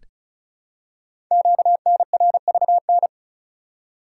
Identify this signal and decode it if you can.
Morse code, heard as a keyed tone, about 36 wpm, 690 Hz. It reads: QNRVN
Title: Code Groups - 5 Letters - 35wpm